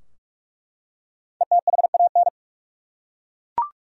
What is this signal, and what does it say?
Morse code, heard as a keyed tone, about 41 wpm, 700 Hz. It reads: ETHAN
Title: USA Common Names - Max 5 Letters - 45wpm